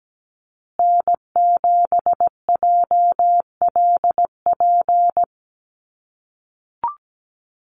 Morse code, heard as a keyed tone, about 17 wpm, 700 Hz. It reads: N7JLP
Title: USA Call Signs Repeated 2x 17wpm